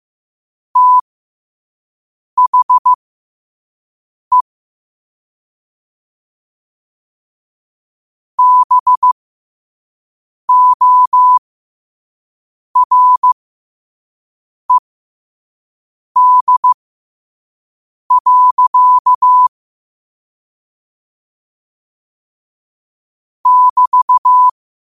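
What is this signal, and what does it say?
Morse code, heard as a keyed tone, about 15 wpm, spaced as for 5 wpm, 1000 Hz. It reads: THE BORED. <BT>
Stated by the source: Morse code practice at 5 wpm.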